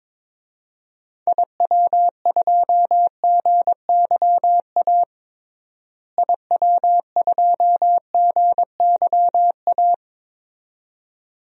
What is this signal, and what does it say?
Morse code, heard as a keyed tone, about 22 wpm, 700 Hz. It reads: IW2GYA IW2GYA
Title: DXCC-100 Call Signs - 6 Characters - Repeated 2x - 22wpm